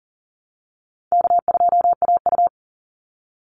Morse code, heard as a keyed tone, about 40 wpm, 700 Hz. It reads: K2AU
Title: Easy Call Signs - 40wpm